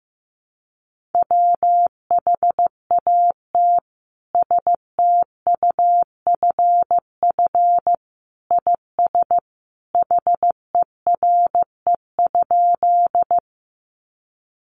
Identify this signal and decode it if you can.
Morse code, heard as a keyed tone, about 15 wpm, 700 Hz. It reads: WHAT STUFF IS HERE?
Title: Sentences from Top 1400 Words 15wpm